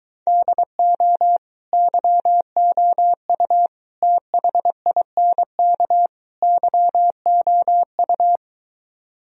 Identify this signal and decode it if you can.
Morse code, heard as a keyed tone, about 23 wpm, 700 Hz. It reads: DO YOU THINK YOU